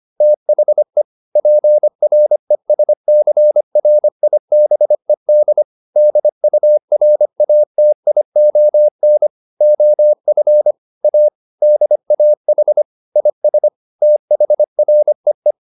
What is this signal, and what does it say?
Morse code, heard as a keyed tone, about 25 wpm, 600 Hz. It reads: THE PRESCRIBED DURATION OF A DAH IS THREE